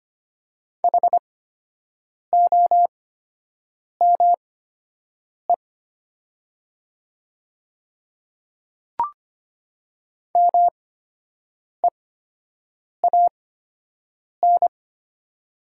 Morse code, heard as a keyed tone, about 25 wpm, spaced as for 7 wpm, 700 Hz. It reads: HOME MEAN
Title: 8x Character Spacing - Top 500 Words - 25wpm